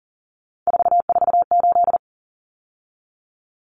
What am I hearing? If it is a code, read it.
Morse code, heard as a keyed tone, about 40 wpm, 700 Hz. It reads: 448
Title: Common Cut Numbers - 3-Digit 40wpm